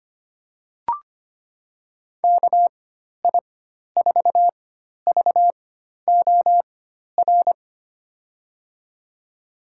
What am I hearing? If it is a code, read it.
Morse code, heard as a keyed tone, about 25 wpm, spaced as for 12 wpm, 700 Hz. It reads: KI4VOR
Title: USA Call Signs - 4x Character Spacing - 25wpm